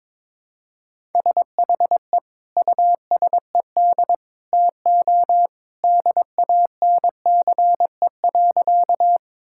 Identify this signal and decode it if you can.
Morse code, heard as a keyed tone, about 22 wpm, 700 Hz. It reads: SHE USED TO DANCE.
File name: practice / 22wpm / Sentences with 2-Word Phrases - 22wpm